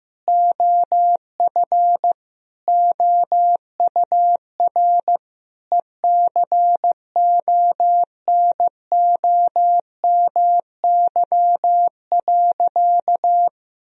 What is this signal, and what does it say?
Morse code, heard as a keyed tone, about 15 wpm, 700 Hz. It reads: OF OUR ECONOMY.
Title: Sentences from Top 1000 Words - Review Part 3 - 15wpm